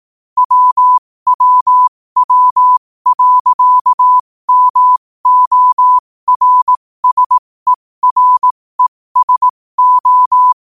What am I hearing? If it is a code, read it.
Morse code, heard as a keyed tone, about 18 wpm, spaced as for 15 wpm, 1000 Hz. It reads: WWW.MORSERESO
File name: qotdfiles / 15wpm